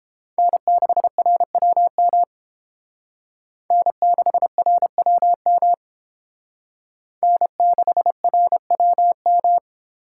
Morse code, heard as a keyed tone, about 31 wpm, 700 Hz. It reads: N6RWM N6RWM N6RWM